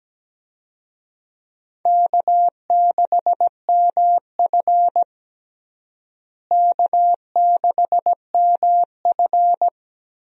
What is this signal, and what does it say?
Morse code, heard as a keyed tone, about 17 wpm, 700 Hz. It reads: K6MF K6MF
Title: USA Call Signs - 4 Characters - Repeated 3x - 17wpm